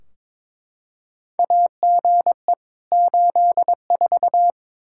Morse code, heard as a keyed tone, about 22 wpm, 700 Hz. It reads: AGE 84
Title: Practice QSO - Exchange 3 - More Info - 22wpm